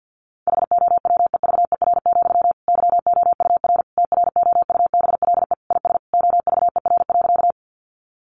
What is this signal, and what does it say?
Morse code, heard as a keyed tone, about 50 wpm, 700 Hz. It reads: HOWEVER, YOUR TROUBLE IS OVER.
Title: Sentences from Top 1200 Words 50wpm